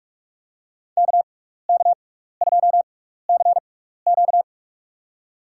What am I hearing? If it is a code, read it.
Morse code, heard as a keyed tone, about 46 wpm, spaced as for 15 wpm, 690 Hz. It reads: KK2CQ
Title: USA Call Signs - 5 Characters - 6x Character Spacing - 45wpm